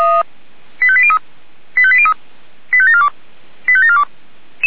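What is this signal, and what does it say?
Five-tone selective call (ZVEI), heard as quick runs of five tones, tones of about 70 ms. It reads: 85803 85803 85742 85742
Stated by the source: SDR-received ZVEI five-tone call 85719.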